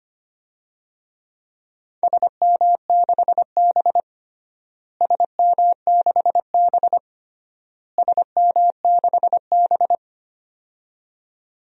Morse code, heard as a keyed tone, about 25 wpm, 700 Hz. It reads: SM6B SM6B SM6B
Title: DXCC-100 Call Signs - 4 Characters - Repeated 3x - 25wpm